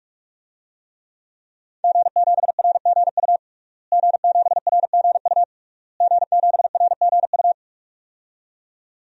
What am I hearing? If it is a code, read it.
Morse code, heard as a keyed tone, about 46 wpm, 700 Hz. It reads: G7RGU G7RGU G7RGU
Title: DXCC-100 Call Signs - 5 Characters - Repeated 3x - 45wpm